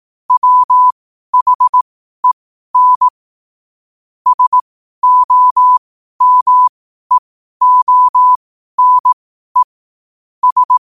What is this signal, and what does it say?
Morse code, heard as a keyed tone, about 18 wpm, spaced as for 12 wpm, 1000 Hz. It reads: WHEN SOMEONE S
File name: qotdfiles / 12wpm